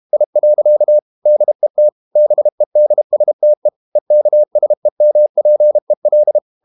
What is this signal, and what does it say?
Morse code, heard as a keyed tone, about 32 wpm, 600 Hz. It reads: I. DET BEDSTE EKSEMPEL
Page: Ferie 32 wpm